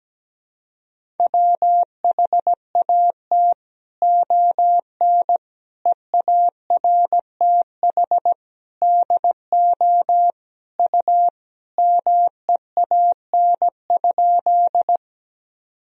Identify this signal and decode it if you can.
Morse code, heard as a keyed tone, about 17 wpm, 700 Hz. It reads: WHAT ON EARTH DO U MEAN?
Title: Sentences from Top 200 Words and Abbreviations 17wpm